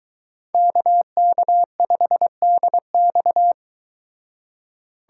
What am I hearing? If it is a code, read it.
Morse code, heard as a keyed tone, about 23 wpm, 700 Hz. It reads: KK5DX